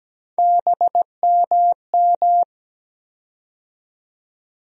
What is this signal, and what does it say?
Morse code, heard as a keyed tone, about 17 wpm, 700 Hz. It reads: BMM
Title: USA Call Sign Postfixes - 3 Letters 17wpm